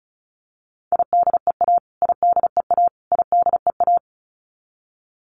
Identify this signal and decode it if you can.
Morse code, heard as a keyed tone, about 34 wpm, 700 Hz. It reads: IDEA IDEA IDEA
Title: Top 500 Words Repeated 35wpm